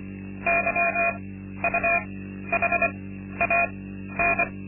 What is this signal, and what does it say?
Morse code, heard as a keyed tone, about 24 wpm, spaced as for 12 wpm, 680 Hz. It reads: YUHAN
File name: Linksmakalnis-2